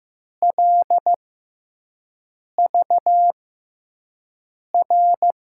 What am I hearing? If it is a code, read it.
Morse code, heard as a keyed tone, about 15 wpm, spaced as for 5 wpm, 700 Hz. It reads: LVR
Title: USA Call Signs - 6 Characters - 6x Character Spacing - 15wpm